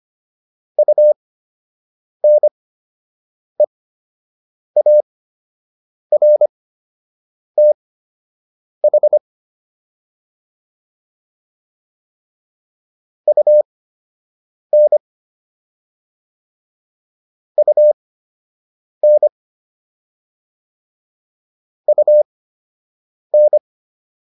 Morse code, heard as a keyed tone, about 25 wpm, spaced as for 7 wpm, 600 Hz. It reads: UNEARTH UN UN UN